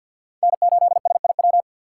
Morse code, heard as a keyed tone, about 47 wpm, 700 Hz. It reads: N8SIW
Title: USA Call Signs - 5 Characters - 3x Character Spacing - 50wpm